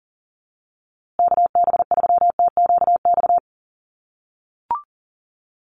Morse code, heard as a keyed tone, about 40 wpm, 700 Hz. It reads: KB3TQX
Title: USA Call Signs Repeated 2x 40wpm